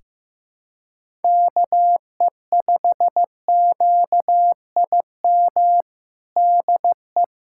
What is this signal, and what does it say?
Morse code, heard as a keyed tone, about 15 wpm, 700 Hz. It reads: KE5QIM DE
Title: USA Callsign DE Callsign - 6 Characters - 15wpm